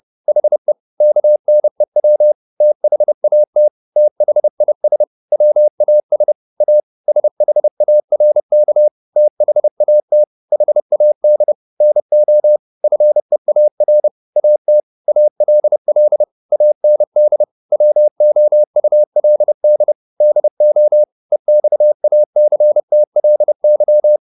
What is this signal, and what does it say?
Morse code, heard as a keyed tone, about 30 wpm, 600 Hz. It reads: HE KNEW THAT THIS WAS A SHARK THAT HAD NO FEAR AT ALL AND WOULD DO EXACTLY